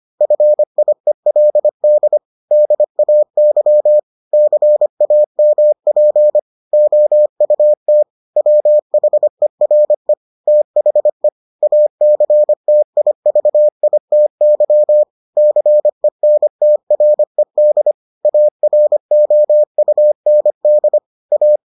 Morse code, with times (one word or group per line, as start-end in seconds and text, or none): 0.2-2.2 FIELD
2.5-4.0 DAY
4.3-6.4 CAMP
6.7-8.0 OUT
8.4-10.1 WHERE
10.5-11.3 THE
11.6-15.0 ACTIVITY
15.4-17.9 CENTRED
18.2-21.0 AROUND
21.3-21.6 A